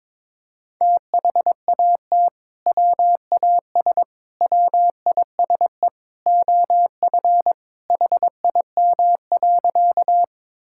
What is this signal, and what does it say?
Morse code, heard as a keyed tone, about 22 wpm, 700 Hz. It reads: THAT WAS WISE OF HIM.
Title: Sentences from Top 1600 Words 22wpm